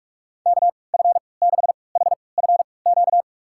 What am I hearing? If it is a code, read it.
Morse code, heard as a keyed tone, about 46 wpm, spaced as for 26 wpm, 700 Hz. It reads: KF6HFQ
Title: USA Call Signs - 3x Character Spacing - 45wpm